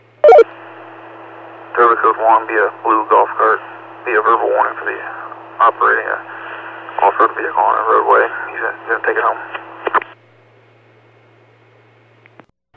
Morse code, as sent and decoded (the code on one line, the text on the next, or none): none